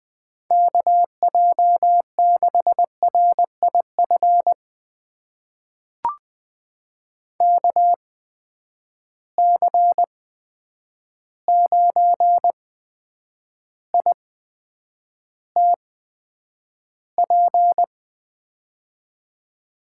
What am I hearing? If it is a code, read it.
Morse code, heard as a keyed tone, about 20 wpm, spaced as for 9 wpm, 700 Hz. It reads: KJ6RIF KC9ITP
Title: USA Call Signs - 6 Characters - 8x Character Spacing - 20wpm